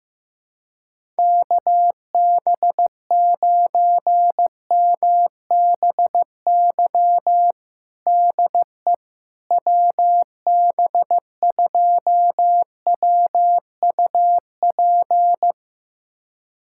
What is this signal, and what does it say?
Morse code, heard as a keyed tone, about 15 wpm, 700 Hz. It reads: KB9MBY DE WB2WUP
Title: USA Callsign DE Callsign - 6 Characters - 15wpm